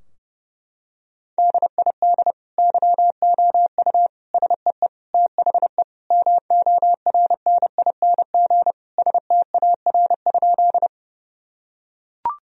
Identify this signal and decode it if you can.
Morse code, heard as a keyed tone, about 30 wpm, 700 Hz. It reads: DID YOU SEE THE MORNING STAR?
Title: Sentences from Top 1000 Words - Review Part 2 - 30wpm